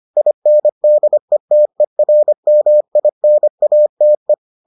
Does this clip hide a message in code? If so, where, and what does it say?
Yes: Morse code from 0.2 to 4.3 s, heard as a keyed tone, about 25 wpm, 600 Hz. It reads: INDETERMINATE